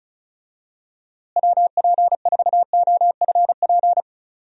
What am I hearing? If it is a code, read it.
Morse code, heard as a keyed tone, about 36 wpm, 700 Hz. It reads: WP4OFP
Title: USA Call Signs - 6 Characters - Repeated 2x - 35wpm